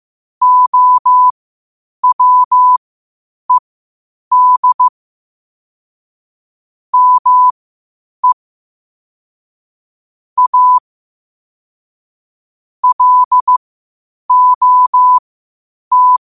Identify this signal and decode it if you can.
Morse code, heard as a keyed tone, about 15 wpm, spaced as for 8 wpm, 1000 Hz. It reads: OWED ME A LOT
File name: qotdfiles / 8wpm